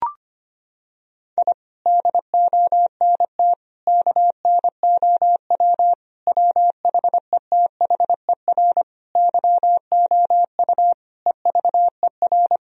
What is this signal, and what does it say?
Morse code, heard as a keyed tone, about 25 wpm, 700 Hz. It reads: I DONT KNOW WHETHER YOU EVER
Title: Sentences from Top 1000 Words - Review Part 3 - 25wpm